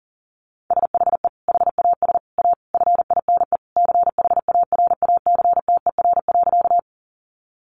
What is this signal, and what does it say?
Morse code, heard as a keyed tone, about 40 wpm, 700 Hz. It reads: SHE HAS A FINE CHARACTER.
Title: Sentences from Top 1200 Words 40wpm